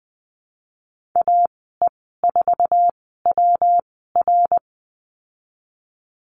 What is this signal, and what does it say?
Morse code, heard as a keyed tone, about 20 wpm, spaced as for 15 wpm, 700 Hz. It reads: AE4WR
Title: 128 Lesson 21g - Number 6 - Call Signs 20wpm